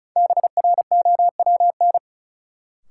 Morse code, heard as a keyed tone, about 36 wpm, 700 Hz. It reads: BROWN
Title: Sentences from Top 1200 Words - New Word Drill - 35wpm